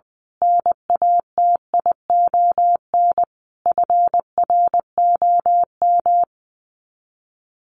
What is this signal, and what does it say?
Morse code, heard as a keyed tone, about 20 wpm, 700 Hz. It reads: NATION FROM
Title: Sets of 5 Words 20wpm